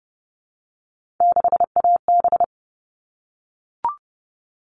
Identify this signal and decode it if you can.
Morse code, heard as a keyed tone, about 30 wpm, 700 Hz. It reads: 6AB
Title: All Cut Numbers - 3-Digit 30wpm